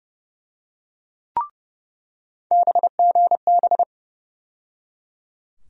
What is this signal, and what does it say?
Morse code, heard as a keyed tone, about 30 wpm, 700 Hz. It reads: BGB
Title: USA Call Sign Postfixes - 3 Letters 30wpm